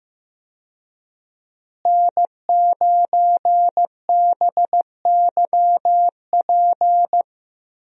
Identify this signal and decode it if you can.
Morse code, heard as a keyed tone, about 15 wpm, 700 Hz. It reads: N9BYP
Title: USA Call Signs - 5 Characters - Repeated 2x - 15wpm